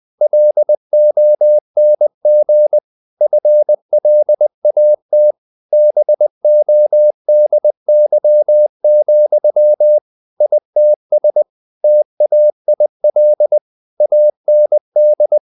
Morse code, heard as a keyed tone, about 20 wpm, 600 Hz. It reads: LONG FLAT BODY, ITS TAIL AND
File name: the old man and the sea part 2 20wpm